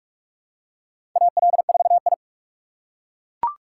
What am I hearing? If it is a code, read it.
Morse code, heard as a keyed tone, about 41 wpm, 700 Hz. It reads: AL4I
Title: USA Call Signs - 4 Characters - 2x Character Spacing - 45wpm